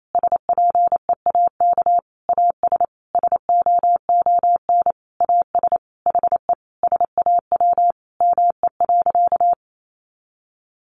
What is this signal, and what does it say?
Morse code, heard as a keyed tone, about 28 wpm, 700 Hz. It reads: SPEAK AS SOON AS HE SAW ME.
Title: Sentences from Top 1100 Words 28wpm